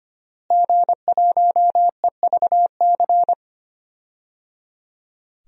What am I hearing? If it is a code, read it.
Morse code, heard as a keyed tone, about 25 wpm, 700 Hz. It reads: G1EVC
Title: DXCC-100 Call Signs - 5 Characters - Repeated 3x - 25wpm